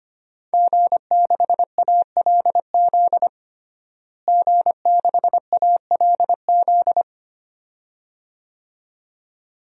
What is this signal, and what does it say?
Morse code, heard as a keyed tone, about 25 wpm, 700 Hz. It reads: G6ALZ G6ALZ
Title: DXCC-100 Call Signs - 5 Characters - Repeated 2x - 25wpm